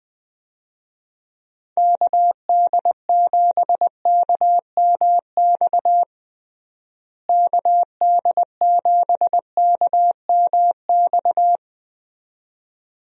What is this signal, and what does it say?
Morse code, heard as a keyed tone, about 20 wpm, 700 Hz. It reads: KD7KMX KD7KMX
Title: USA Call Signs - 6 Characters - Repeated 2x - 20wpm